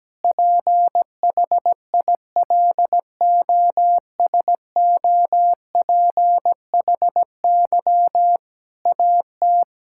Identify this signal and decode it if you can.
Morse code, heard as a keyed tone, about 17 wpm, 700 Hz. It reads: PHILOSOPHY AT
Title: Sentences from Top 1000 Words - Review Part 6 - 17wpm